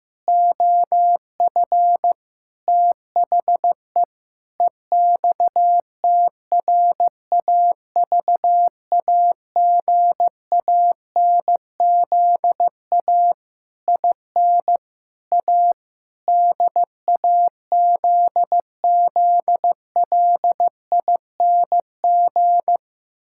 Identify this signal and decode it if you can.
Morse code, heard as a keyed tone, about 15 wpm, 700 Hz. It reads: OF THE EXTRAVAGANZA IN A DAZZLING
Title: Pangram Challenge v2 - 15wpm